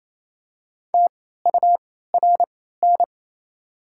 Morse code, heard as a keyed tone, about 27 wpm, spaced as for 16 wpm, 700 Hz. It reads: TURN
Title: 3x Character Spacing - Top 300 Words - 28wpm